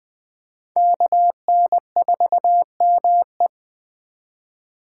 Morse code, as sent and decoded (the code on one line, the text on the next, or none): -.- -. ....- -- .
KN4ME